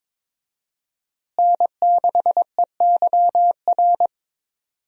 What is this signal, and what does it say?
Morse code, heard as a keyed tone, about 22 wpm, 700 Hz. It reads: N6EYR
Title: USA Call Signs - 5 Characters - Repeated 3x - 22wpm